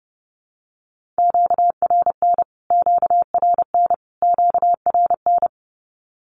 Morse code, heard as a keyed tone, about 30 wpm, 700 Hz. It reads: QRN QRN QRN
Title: Q-Codes Repeated 3x 30wpm